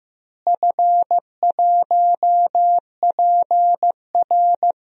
Morse code, heard as a keyed tone, about 15 wpm, 700 Hz. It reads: F1PR
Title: DXCC-100 Call Signs - 4 Characters - Repeated 3x - 15wpm